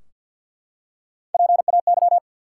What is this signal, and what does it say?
Morse code, heard as a keyed tone, about 53 wpm, 700 Hz. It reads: PAY
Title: Three Letter Words - 50wpm